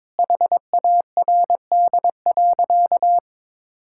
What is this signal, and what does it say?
Morse code, heard as a keyed tone, about 22 wpm, 700 Hz. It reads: HARD.
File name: practice / 22wpm / Sentences with 3-Word Phrases - 22wpm